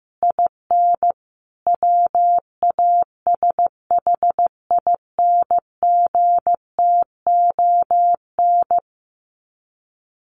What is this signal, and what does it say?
Morse code, heard as a keyed tone, about 15 wpm, 700 Hz. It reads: IN WASHINGTON
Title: Sets of 5 Words Encore2 15wpm